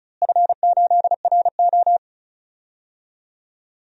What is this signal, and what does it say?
Morse code, heard as a keyed tone, about 36 wpm, 700 Hz. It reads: F8RO
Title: DXCC-100 Call Signs - 4 Characters - Repeated 3x - 35wpm